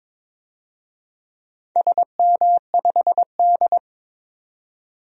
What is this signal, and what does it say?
Morse code, heard as a keyed tone, about 22 wpm, 700 Hz. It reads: SM5D
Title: DXCC-100 Call Signs - 4 Characters - 22wpm